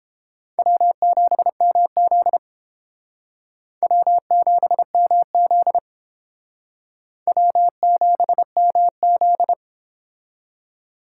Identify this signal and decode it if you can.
Morse code, heard as a keyed tone, about 30 wpm, 700 Hz. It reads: W7MZ W7MZ W7MZ